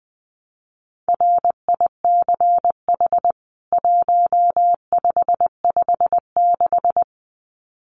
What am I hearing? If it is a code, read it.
Morse code, heard as a keyed tone, about 20 wpm, 700 Hz. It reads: RICH 1556
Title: CWT Exchange 20wpm